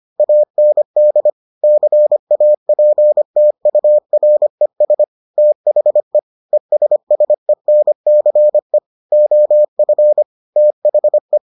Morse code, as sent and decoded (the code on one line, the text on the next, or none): .- -. -.. / -.-. .- .--. - ..- .-. ... / - .... / . ... ... -. -.-. / --- ..-. / - .... .
AND CAPTURES THE ESSENCE OF THE